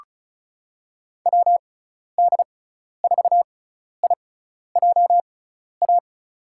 Morse code, heard as a keyed tone, about 34 wpm, spaced as for 12 wpm, 700 Hz. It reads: WD4IJA
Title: USA Call Signs - 6 Characters - 6x Character Spacing - 35wpm